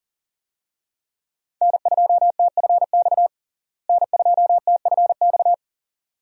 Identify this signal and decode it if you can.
Morse code, heard as a keyed tone, about 40 wpm, 700 Hz. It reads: N2TFX N2TFX
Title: USA Call Signs - 5 Characters - Repeated 2x - 40wpm